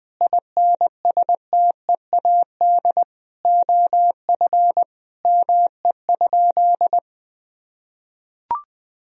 Morse code, heard as a keyed tone, about 21 wpm, 700 Hz. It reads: INSTEAD OF ME?
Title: Sentences from Top 1000 Words - Review Part 4 - 20wpm